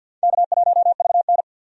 Morse code, heard as a keyed tone, about 48 wpm, 700 Hz. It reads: K1VN
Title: USA Call Signs - 4 Characters - 50wpm